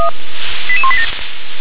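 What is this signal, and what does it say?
Five-tone selective call (ZVEI), heard as quick runs of five tones, tones of about 70 ms. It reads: 88187
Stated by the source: ZVEI five-tone selective call received via SDR.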